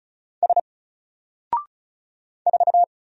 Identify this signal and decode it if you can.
Morse code, heard as a keyed tone, about 36 wpm, 700 Hz. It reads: S 4
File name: Single-Letter-Number-35wpm